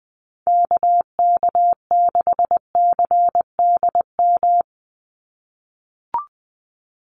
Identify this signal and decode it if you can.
Morse code, heard as a keyed tone, about 20 wpm, 700 Hz. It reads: KK6CDM
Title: USA Call Signs 20wpm